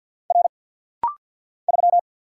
Morse code, heard as a keyed tone, about 50 wpm, 690 Hz. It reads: R 3
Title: Single Letter-Number Alt - Rapid-Fire 50wpm